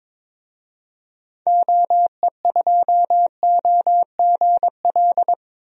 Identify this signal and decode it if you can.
Morse code, heard as a keyed tone, about 22 wpm, 700 Hz. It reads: OE2OGL
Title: DXCC-100 Call Signs - 6 Characters - Repeated 3x - 22wpm